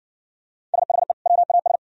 Morse code, heard as a keyed tone, about 56 wpm, 690 Hz. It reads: SHE WAS